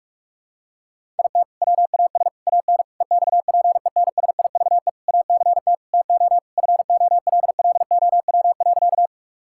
Morse code, heard as a keyed tone, about 46 wpm, 700 Hz. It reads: IT WAS AN EXPENSIVE ACT TO FOLLOW.
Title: Sentences from Top 1000 Words - Review Part 6 - 45wpm